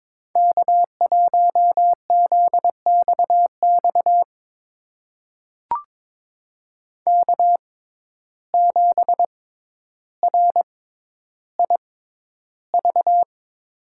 Morse code, heard as a keyed tone, about 22 wpm, spaced as for 11 wpm, 700 Hz. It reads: K1ZXX K7RIV
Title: USA Call Signs - 5 Characters - 6x Character Spacing - 22wpm